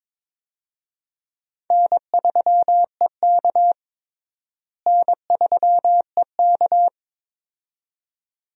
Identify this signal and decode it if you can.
Morse code, heard as a keyed tone, about 22 wpm, 700 Hz. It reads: N3EK N3EK
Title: USA Call Signs - 4 Characters - Repeated 2x - 22wpm